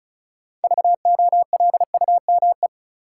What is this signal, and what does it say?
Morse code, heard as a keyed tone, about 34 wpm, 700 Hz. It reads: VOLUME
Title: Six Letter Words - 35wpm